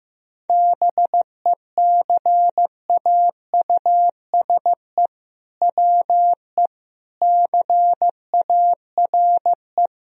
Morse code, heard as a keyed tone, about 15 wpm, 700 Hz. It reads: BECAUSE WE CARE
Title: Sentences from Top 1000 Words - Review Part 3 - 15wpm